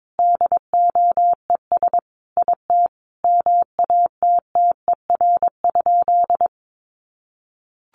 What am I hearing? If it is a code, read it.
Morse code, heard as a keyed tone, about 22 wpm, 700 Hz. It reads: DOES IT MATTER?
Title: Sentences from Top 700 Words 22wpm